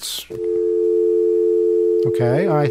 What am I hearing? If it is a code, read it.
Telephone call progress tone (North America), heard as a telephone network's tone, about 350, 440 Hz, with speech. Dial tone.